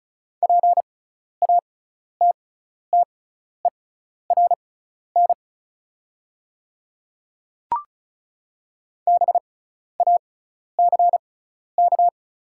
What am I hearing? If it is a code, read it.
Morse code, heard as a keyed tone, about 36 wpm, spaced as for 12 wpm, 690 Hz. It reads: PATTERN BACK
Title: 6x Character Spacing - Top 500 Words - 35wpm